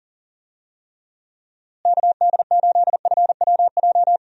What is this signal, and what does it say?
Morse code, heard as a keyed tone, about 40 wpm, 700 Hz. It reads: KD8FWJ
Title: USA Call Signs - 6 Characters - Repeated 3x - 40wpm